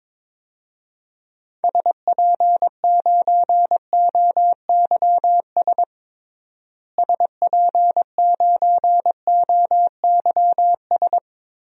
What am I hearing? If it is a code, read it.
Morse code, heard as a keyed tone, about 22 wpm, 700 Hz. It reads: SP9OYS SP9OYS
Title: DXCC-100 Call Signs - 6 Characters - Repeated 2x - 22wpm